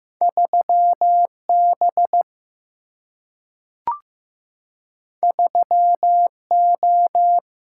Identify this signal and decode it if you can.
Morse code, heard as a keyed tone, about 15 wpm, 700 Hz. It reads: 3B 3O
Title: Single-Letter Single-Number 15wpm